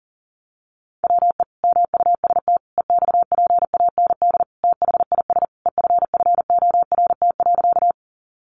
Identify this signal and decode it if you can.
Morse code, heard as a keyed tone, about 40 wpm, 700 Hz. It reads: WE MUST EXPAND THIS EFFORT.
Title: Sentences from Top 1400 Words 40wpm